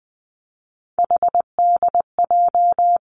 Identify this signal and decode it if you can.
Morse code, heard as a keyed tone, about 20 wpm, 700 Hz. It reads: HDJ
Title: Three Challenging Letter-Number Combinations 20wpm